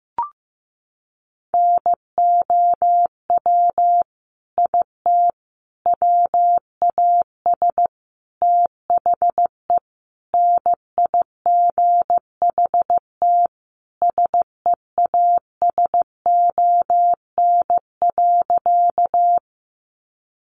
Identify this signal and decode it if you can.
Morse code, heard as a keyed tone, about 15 wpm, 700 Hz. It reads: NOW IT WAS THE NIGHT SEASON.
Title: Sentences from Top 1400 Words 15wpm